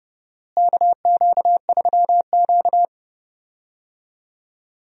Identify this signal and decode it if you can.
Morse code, heard as a keyed tone, about 31 wpm, 700 Hz. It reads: KQ3Q
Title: USA Call Signs - 4 Characters - Repeated 3x - 30wpm